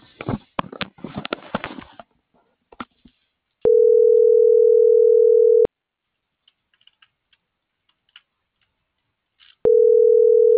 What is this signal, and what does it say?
Ringback tone (North America), 440 + 480 Hz.